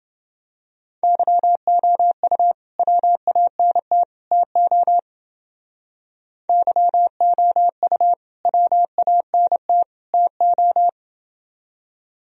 Morse code, heard as a keyed tone, about 28 wpm, 700 Hz. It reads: YOU WANT TO YOU WANT TO